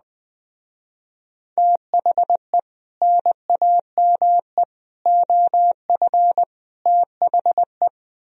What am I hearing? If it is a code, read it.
Morse code, heard as a keyed tone, about 20 wpm, 700 Hz. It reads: THE NAME OF THE
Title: Speed-Racing - Sets of 4 Words 17wpm